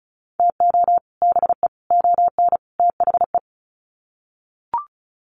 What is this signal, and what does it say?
Morse code, heard as a keyed tone, about 34 wpm, 700 Hz. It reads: TO BE ON THE
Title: Sets of 4 Words 35wpm